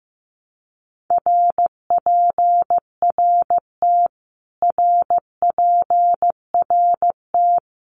Morse code, heard as a keyed tone, about 15 wpm, 700 Hz. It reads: RPRT RPRT
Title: Abbreviations Repeated 3x - 15wpm